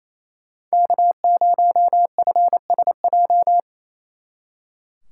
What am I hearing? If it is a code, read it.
Morse code, heard as a keyed tone, about 36 wpm, 700 Hz. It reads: K0FSJ